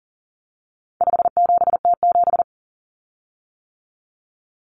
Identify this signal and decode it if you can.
Morse code, heard as a keyed tone, about 40 wpm, 700 Hz. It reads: 57T7
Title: Common Cut Numbers - 4-Digit 40wpm